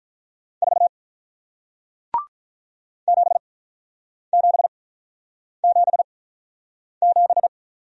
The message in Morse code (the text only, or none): 4 7777